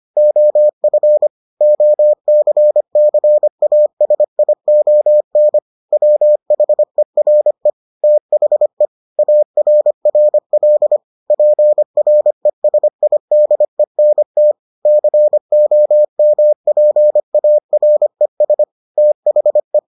Morse code, heard as a keyed tone, about 25 wpm, 600 Hz. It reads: OF OCCASION WHERE THE ARRL PRESIDENT COMPARES THE